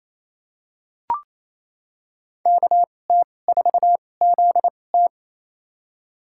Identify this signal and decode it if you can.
Morse code, heard as a keyed tone, about 28 wpm, spaced as for 20 wpm, 700 Hz. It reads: KT4ZT
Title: USA Call Signs - 5 Characters - 2x Character Spacing - 28wpm